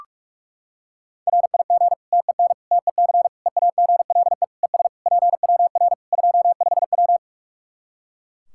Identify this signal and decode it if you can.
Morse code, heard as a keyed tone, about 43 wpm, 700 Hz. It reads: RIG TEN TEC EAGLE ES PWR 25W
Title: K1USN SST QSO Practice 45wpm